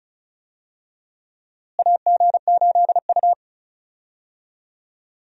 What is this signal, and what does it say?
Morse code, heard as a keyed tone, about 34 wpm, 700 Hz. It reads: AG8U
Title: USA Call Signs - 4 Characters - Repeated 3x - 35wpm